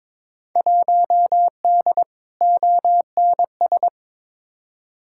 Morse code, heard as a keyed tone, about 22 wpm, 700 Hz. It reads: 1D ONS
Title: Field Day Exchanges - 22wpm